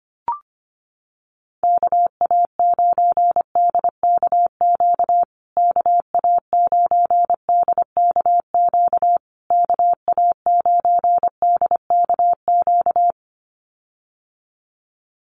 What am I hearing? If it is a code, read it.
Morse code, heard as a keyed tone, about 25 wpm, 700 Hz. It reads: KA9DKQ KA9DKQ KA9DKQ
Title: USA Call Signs Repeated 3x 25wpm